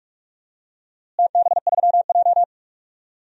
Morse code, heard as a keyed tone, about 42 wpm, 700 Hz. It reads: TB3J
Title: DXCC-100 Call Signs - 4 Characters - Repeated 3x - 45wpm